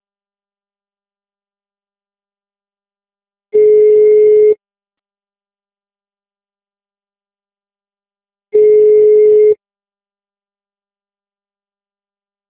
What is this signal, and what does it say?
Ringback tone (Europe, 425 Hz).
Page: Meier - RJ